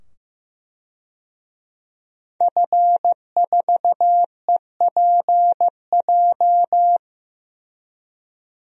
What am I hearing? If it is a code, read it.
Morse code, heard as a keyed tone, about 16 wpm, 700 Hz. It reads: F4EPJ
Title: DXCC-100 Call Signs - 5 Characters - 15wpm